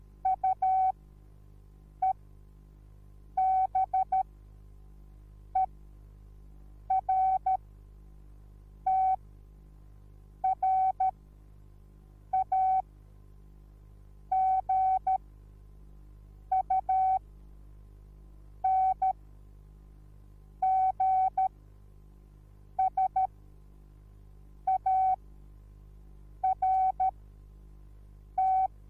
Morse code, heard as a keyed tone, about 13 wpm, spaced as for 5 wpm, 760 Hz. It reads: UEBERTRAGUNGSART